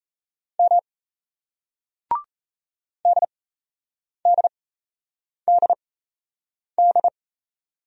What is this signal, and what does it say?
Morse code, heard as a keyed tone, about 38 wpm, spaced as for 8 wpm, 700 Hz. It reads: M DDDD